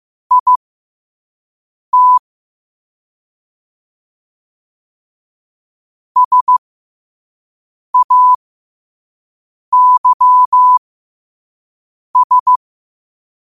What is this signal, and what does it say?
Morse code, heard as a keyed tone, about 15 wpm, spaced as for 5 wpm, 1000 Hz. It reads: IT SAYS
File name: qotdfiles / 5wpm